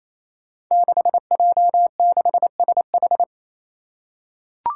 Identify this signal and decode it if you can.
Morse code, heard as a keyed tone, about 28 wpm, 700 Hz. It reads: 6J6SH
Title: Five Challenging Letter-Number Combinations 28wpm